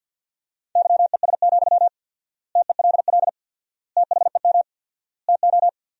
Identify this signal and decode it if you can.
Morse code, heard as a keyed tone, about 52 wpm, 700 Hz. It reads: YES, TELL THEM TO
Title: 4x Word-Spacing - Sentences from Top 500 Words - 50wpm